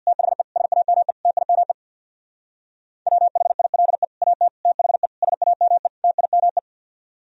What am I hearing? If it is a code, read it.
Morse code, heard as a keyed tone, about 63 wpm, 700 Hz. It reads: THE SAME TIME WHILE AT THE SAME TIME